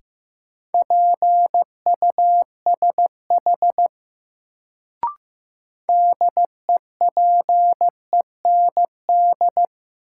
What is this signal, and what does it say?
Morse code, heard as a keyed tone, about 15 wpm, 700 Hz. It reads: PUSH DEPEND
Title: Sentences from Top 1300 Words - New Word Drill Alt - 15wpm